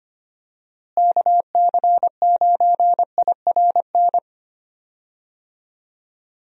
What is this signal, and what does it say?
Morse code, heard as a keyed tone, about 25 wpm, 700 Hz. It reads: KC9IRN
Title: USA Call Signs - 6 Characters - Repeated 3x - 25wpm